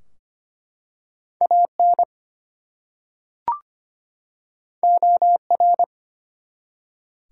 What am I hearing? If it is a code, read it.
Morse code, heard as a keyed tone, about 25 wpm, 700 Hz. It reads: AN OR
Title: Two Letter Words - 25wpm